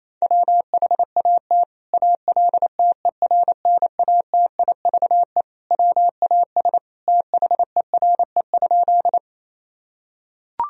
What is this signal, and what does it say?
Morse code, heard as a keyed tone, about 28 wpm, 700 Hz. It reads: WHAT ALTERNATIVE WAS THERE?
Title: Sentences from Top 1000 Words - Review Part 4 - 28wpm